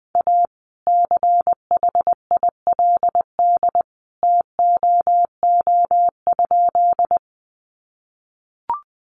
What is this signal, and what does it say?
Morse code, heard as a keyed tone, about 20 wpm, 700 Hz. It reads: A CHILD TOO?
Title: Sentences from Top 800 Words 20wpm